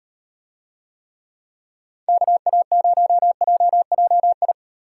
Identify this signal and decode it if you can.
Morse code, heard as a keyed tone, about 36 wpm, 700 Hz. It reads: KA0JJI